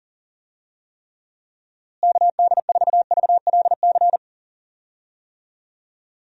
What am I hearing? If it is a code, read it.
Morse code, heard as a keyed tone, about 40 wpm, 700 Hz. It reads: KD4VLC
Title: USA Call Signs - 6 Characters - Repeated 2x - 40wpm